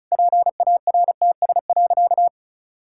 Morse code, heard as a keyed tone, about 34 wpm, 700 Hz. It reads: PARTS.
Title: Sentences from Top 1000 Words - Review Part 6 - 35wpm